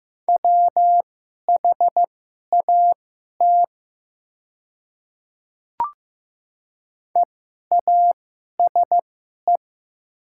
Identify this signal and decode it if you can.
Morse code, heard as a keyed tone, about 15 wpm, spaced as for 11 wpm, 700 Hz. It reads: WHAT EASE
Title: 2x Character Spacing - Top 300 Words - 15wpm